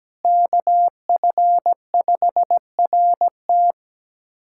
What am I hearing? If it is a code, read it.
Morse code, heard as a keyed tone, about 17 wpm, 700 Hz. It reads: KF5RT
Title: USA Callsign DE Callsign - 5 Characters - 17wpm